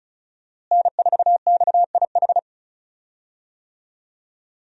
Morse code, heard as a keyed tone, about 35 wpm, 700 Hz. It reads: N4XIH
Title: USA Call Signs - 5 Characters - Repeated 2x - 35wpm